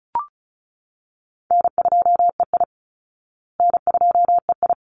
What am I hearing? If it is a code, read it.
Morse code, heard as a keyed tone, about 34 wpm, 700 Hz. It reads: N2EI N2EI
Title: Easy Call Signs Repeated 2x - 35wpm